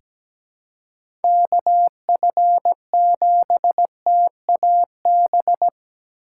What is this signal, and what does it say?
Morse code, heard as a keyed tone, about 17 wpm, 700 Hz. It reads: KF7TAB
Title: USA Call Signs - 6 Characters - 6x Character Spacing - 17wpm